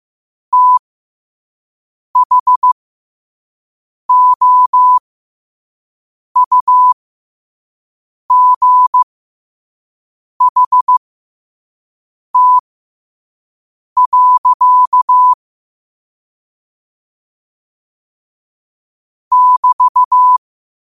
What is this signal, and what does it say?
Morse code, heard as a keyed tone, about 15 wpm, spaced as for 5 wpm, 1000 Hz. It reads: THOUGHT. <BT>
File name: 5wpm